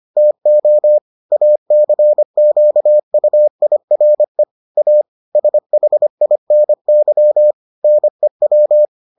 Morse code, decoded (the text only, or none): TO ACQUIRE A SHINY NEW